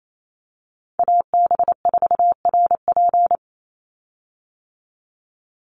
Morse code, heard as a keyed tone, about 28 wpm, 700 Hz. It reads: AB4RP
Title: USA Call Signs 28wpm